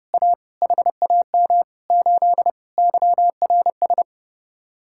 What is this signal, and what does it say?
Morse code, heard as a keyed tone, about 30 wpm, 700 Hz. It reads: A HAM 8 YRS